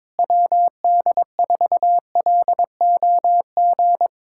Morse code, heard as a keyed tone, about 22 wpm, 700 Hz. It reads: WD4LOG
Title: USA Callsign DE Callsign - 6 Characters - 22wpm